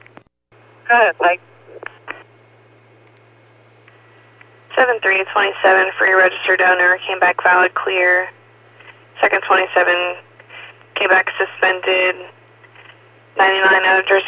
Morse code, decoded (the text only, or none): none